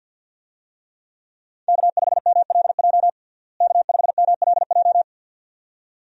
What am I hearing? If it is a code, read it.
Morse code, heard as a keyed tone, about 51 wpm, 700 Hz. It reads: K5MLJ K5MLJ
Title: USA Call Signs - 5 Characters - Repeated 2x - 50wpm